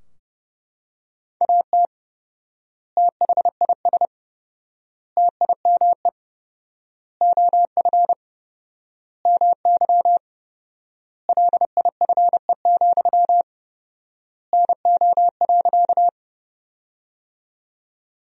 Morse code, heard as a keyed tone, about 30 wpm, 700 Hz. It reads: AT THIS TIME OF MY LIFE, NO.
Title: 4x Word-Spacing - Sentences from Top 300 Words - 30wpm